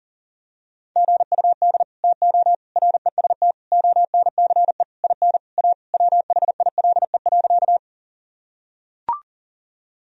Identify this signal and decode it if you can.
Morse code, heard as a keyed tone, about 37 wpm, 700 Hz. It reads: GUD TO REST ONCE IN A WHILE.
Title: Sentences from Top 500 Words and Abbreviations 40wpm